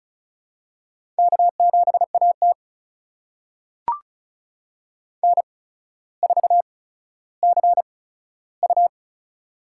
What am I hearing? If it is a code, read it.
Morse code, heard as a keyed tone, about 36 wpm, spaced as for 15 wpm, 700 Hz. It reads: K7AT N4CU
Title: USA Call Signs - 4 Characters - 8x Character Spacing - 35wpm